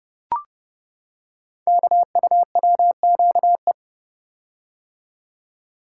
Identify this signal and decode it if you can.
Morse code, heard as a keyed tone, about 30 wpm, 700 Hz. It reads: KUWQE
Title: Code Groups - 5 Letters - 30wpm